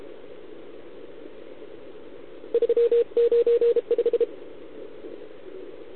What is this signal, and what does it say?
Morse code, heard as a keyed tone, about 33 wpm, 470 Hz. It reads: 395